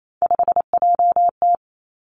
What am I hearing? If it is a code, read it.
Morse code, heard as a keyed tone, about 28 wpm, 700 Hz. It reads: 5JT
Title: Three Challenging Letter-Number Combinations 28wpm